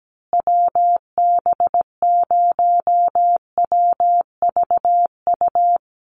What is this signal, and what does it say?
Morse code, heard as a keyed tone, about 17 wpm, 700 Hz. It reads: WB0WVU
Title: USA Call Signs Repeated 2x 17wpm